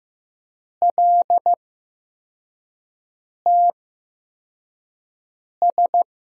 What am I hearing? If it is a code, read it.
Morse code, heard as a keyed tone, about 15 wpm, spaced as for 4 wpm, 700 Hz. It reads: LTS
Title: USA Call Signs - 6 Characters - 8x Character Spacing - 15wpm